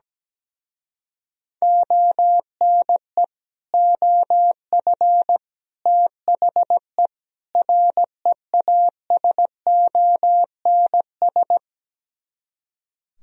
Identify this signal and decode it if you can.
Morse code, heard as a keyed tone, about 17 wpm, 700 Hz. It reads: ONE OF THE REASONS
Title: Speed-Racing - Sets of 4 Words 17wpm